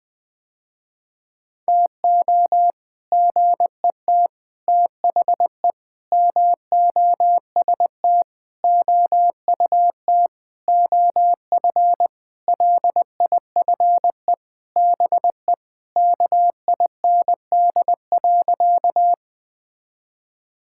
Morse code, heard as a keyed tone, about 20 wpm, 700 Hz. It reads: TO GET THE MOST OUT OF LIFE BE KIND.
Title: Sentences with 6-Word Phrases - Spoken Beginning - 20wpm